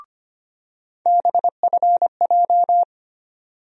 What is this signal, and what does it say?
Morse code, heard as a keyed tone, about 25 wpm, 700 Hz. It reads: BFJ
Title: USA Call Sign Postfixes - 3 Letters 25wpm